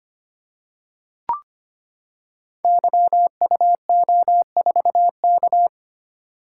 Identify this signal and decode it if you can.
Morse code, heard as a keyed tone, about 25 wpm, 700 Hz. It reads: YUO4K
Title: Code Groups - 5 Letter:Numbers - 25wpm